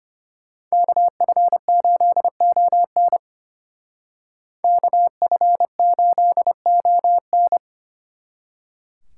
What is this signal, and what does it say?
Morse code, heard as a keyed tone, about 31 wpm, 700 Hz. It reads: KF8ON KF8ON
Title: Speed Racing - USA 2x2 Call Signs 25wpm